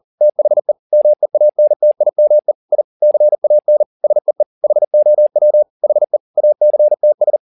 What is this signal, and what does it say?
Morse code, heard as a keyed tone, about 40 wpm, 600 Hz. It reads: THE MEANTIME I CAN SEE HOW HE ACTS
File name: the old man and the sea 2 40wpm